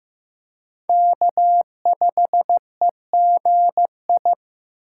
Morse code, heard as a keyed tone, about 15 wpm, 700 Hz. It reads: K5EGI
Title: USA Call Signs - 4x Character Spacing - 15wpm